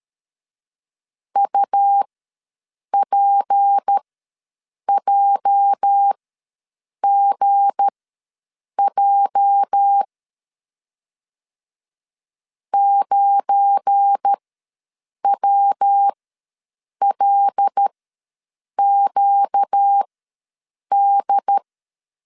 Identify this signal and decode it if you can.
Morse code, heard as a keyed tone, about 13 wpm, spaced as for 7 wpm, 800 Hz. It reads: UPJGJ 9WLQD
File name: LEZIONI / 8 YJ45 / 8 YJ ws 12wpm 3342